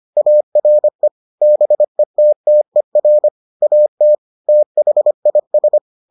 Morse code, heard as a keyed tone, about 25 wpm, 600 Hz. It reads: ARE BETTER AT THIS